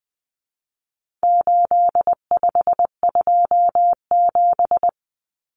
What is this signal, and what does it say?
Morse code, heard as a keyed tone, about 20 wpm, 700 Hz. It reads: 8527
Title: Common Cut Numbers - 4-Digit 20wpm